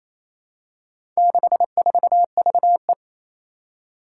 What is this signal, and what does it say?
Morse code, heard as a keyed tone, about 28 wpm, 700 Hz. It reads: 64VE
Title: Four Challenging Letter-Number Combinations 28wpm